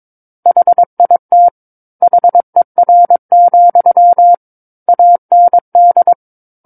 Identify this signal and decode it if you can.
Morse code, heard as a keyed tone, about 22 wpm, 700 Hz. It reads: HIT HER, AND
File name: 06-Chapter-06 22 WPM